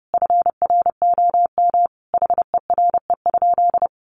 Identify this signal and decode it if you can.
Morse code, heard as a keyed tone, about 30 wpm, 700 Hz. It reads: FROM HERE?
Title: Sentences from Top 1100 Words 30wpm